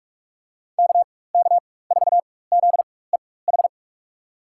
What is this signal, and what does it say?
Morse code, heard as a keyed tone, about 46 wpm, spaced as for 21 wpm, 700 Hz. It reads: KK4ZEH
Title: USA Call Signs - 6 Characters - 4x Character Spacing - 45wpm